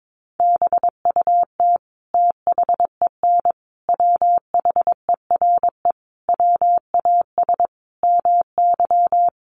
Morse code, heard as a keyed tone, about 22 wpm, 700 Hz. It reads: BUT THEN WHERE WAS MY
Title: Sentences from Top 1300 Words 22wpm